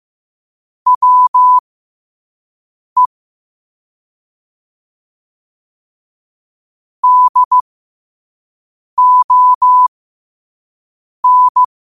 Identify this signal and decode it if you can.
Morse code, heard as a keyed tone, about 15 wpm, spaced as for 5 wpm, 1000 Hz. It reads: WE DON
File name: qotdfiles / 5wpm